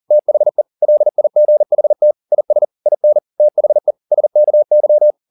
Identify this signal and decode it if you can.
Morse code, heard as a keyed tone, about 40 wpm, 600 Hz. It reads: THE LIGHT IS IN THE SKY
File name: the old man and the sea 2 40wpm